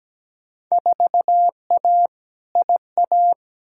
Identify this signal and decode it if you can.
Morse code, heard as a keyed tone, about 17 wpm, 700 Hz. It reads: 4A IA
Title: Field Day Exchanges - 17wpm